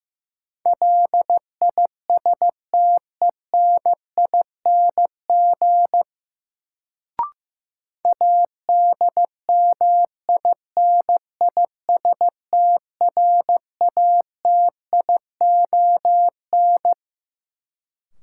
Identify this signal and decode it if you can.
Morse code, heard as a keyed tone, about 15 wpm, 700 Hz. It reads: LISTENING ADMINISTRATION
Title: Sets of 5 Words - New Word Drill - 15wpm